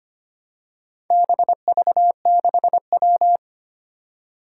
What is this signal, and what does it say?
Morse code, heard as a keyed tone, about 25 wpm, 700 Hz. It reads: BV6W
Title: Four Challenging Letter-Number Combinations 25wpm